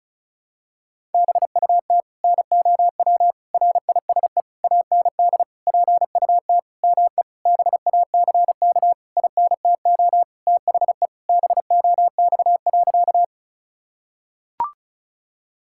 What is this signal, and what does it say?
Morse code, heard as a keyed tone, about 36 wpm, 700 Hz. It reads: BUT NOW RISE AND PUT ME BACK INTO THE BOX.
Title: Sentences from Top 1000 Words - Review Part 6 - 35wpm